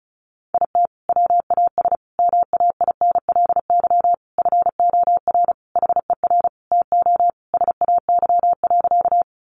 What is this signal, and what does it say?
Morse code, heard as a keyed tone, about 34 wpm, 700 Hz. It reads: IT WAS MAINLY FOR HER TO SAY.
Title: Sentences from Top 1400 Words 35wpm